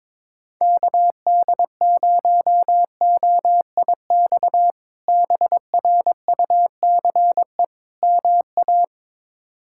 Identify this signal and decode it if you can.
Morse code, heard as a keyed tone, about 22 wpm, 700 Hz. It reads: KD0OIX BRUCE MA
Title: K1USN SST QSO Practice - 22wpm